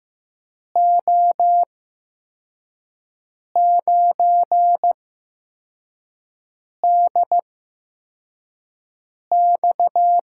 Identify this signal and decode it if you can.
Morse code, heard as a keyed tone, about 15 wpm, spaced as for 4 wpm, 700 Hz. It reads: O9DX
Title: USA Call Signs - 5 Characters - 8x Character Spacing - 15wpm